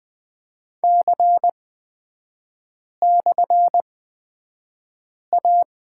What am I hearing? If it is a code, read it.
Morse code, heard as a keyed tone, about 20 wpm, spaced as for 5 wpm, 700 Hz. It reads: C/A
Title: 224 Lesson 36e - BT - Three Characters 20wpm